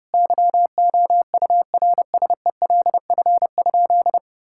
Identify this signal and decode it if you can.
Morse code, heard as a keyed tone, about 30 wpm, 700 Hz. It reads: YOURSELF?
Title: Sentences from Top 1600 Words 30wpm